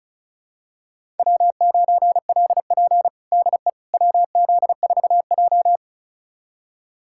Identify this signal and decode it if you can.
Morse code, heard as a keyed tone, about 36 wpm, 700 Hz. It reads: W9LP DE WZ4J
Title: USA Callsign DE Callsign - 4 Characters - 35wpm